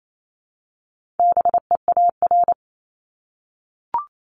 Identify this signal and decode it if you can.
Morse code, heard as a keyed tone, about 28 wpm, 700 Hz. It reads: BEAR
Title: ING Suffix - Top 100 Words with Base Words 28wpm